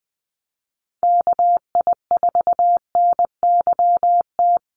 Morse code, heard as a keyed tone, about 20 wpm, 700 Hz. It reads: KI4NYT
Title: USA Call Signs Repeated 2x 20wpm